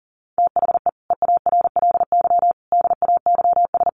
Morse code, heard as a keyed tone, about 40 wpm, 700 Hz. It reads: THE EARLY DAYS
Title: Sets of 5 Words Encore 40wpm